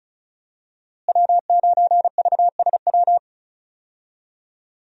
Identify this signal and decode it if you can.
Morse code, heard as a keyed tone, about 34 wpm, 700 Hz. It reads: W9VSW
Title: USA Call Signs - 5 Characters - Repeated 3x - 35wpm